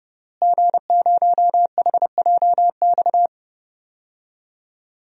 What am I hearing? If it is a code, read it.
Morse code, heard as a keyed tone, about 30 wpm, 700 Hz. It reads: G0HJX
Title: DXCC-100 Call Signs - 5 Characters - Repeated 2x - 30wpm